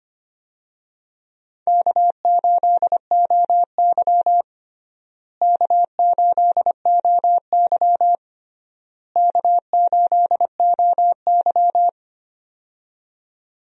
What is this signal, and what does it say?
Morse code, heard as a keyed tone, about 25 wpm, 700 Hz. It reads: K8OY K8OY K8OY
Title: USA Call Signs - 4 Characters - Repeated 3x - 25wpm